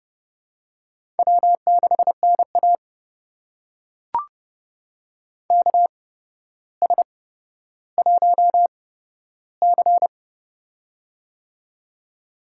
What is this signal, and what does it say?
Morse code, heard as a keyed tone, about 32 wpm, spaced as for 13 wpm, 700 Hz. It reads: W6NA KS1C
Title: USA Call Signs - 4 Characters - 8x Character Spacing - 30wpm